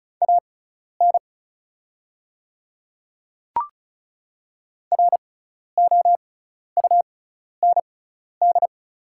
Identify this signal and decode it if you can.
Morse code, heard as a keyed tone, about 36 wpm, spaced as for 12 wpm, 690 Hz. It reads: AN ROUND
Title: 6x Character Spacing - Top 300 Words - 35wpm